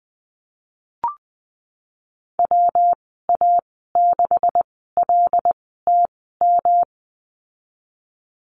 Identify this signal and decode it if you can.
Morse code, heard as a keyed tone, about 20 wpm, spaced as for 15 wpm, 700 Hz. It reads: WA6LTM
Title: 179 Lesson 29g - Slash - Call Signs 20wpm